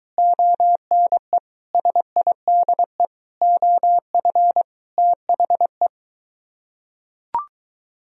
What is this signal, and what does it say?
Morse code, heard as a keyed tone, about 21 wpm, 700 Hz. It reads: ONE SIDE OF THE